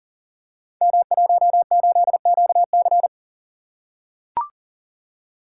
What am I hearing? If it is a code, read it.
Morse code, heard as a keyed tone, about 40 wpm, 700 Hz. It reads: M18QC
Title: Code Groups - 5 Letter:Numbers - 40wpm